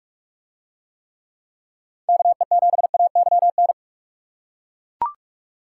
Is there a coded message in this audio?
Yes: Morse code from 2.1 to 3.7 s, heard as a keyed tone, about 42 wpm, 700 Hz. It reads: KE7AYN